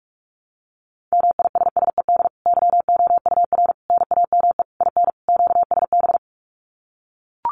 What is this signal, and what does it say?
Morse code, heard as a keyed tone, about 44 wpm, 700 Hz. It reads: MISSED YOUR NAME IN QSB
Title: Common QSO Phrases 45wpm